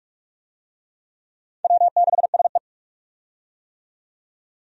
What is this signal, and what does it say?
Morse code, heard as a keyed tone, about 46 wpm, 700 Hz. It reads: W6SE
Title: USA Call Signs - 4 Characters - 45wpm